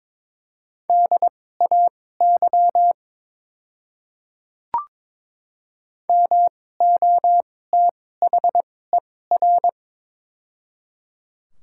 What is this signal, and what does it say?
Morse code, heard as a keyed tone, about 23 wpm, spaced as for 16 wpm, 700 Hz. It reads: DAY MOTHER
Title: 2x Character Spacing - Top 200 Words - 22wpm